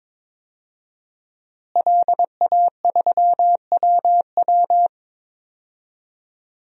Morse code, heard as a keyed tone, about 22 wpm, 700 Hz. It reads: LA3WW
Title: DXCC-100 Call Signs - 5 Characters - Repeated 2x - 22wpm